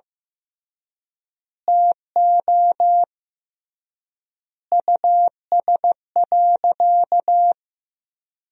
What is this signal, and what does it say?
Morse code, heard as a keyed tone, about 15 wpm, 700 Hz. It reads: TO US.
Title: 3x Word-Spacing - Sentences from Top 200 Words - 15wpm